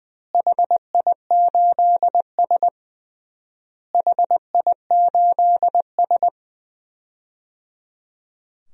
Morse code, heard as a keyed tone, about 21 wpm, 700 Hz. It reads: HI8S HI8S
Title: DXCC-100 Call Signs - 4 Characters - Repeated 3x - 20wpm